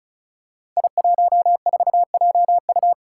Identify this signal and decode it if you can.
Morse code, heard as a keyed tone, about 36 wpm, 700 Hz. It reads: I14JU
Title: Five Challenging Letter-Number Combinations 35wpm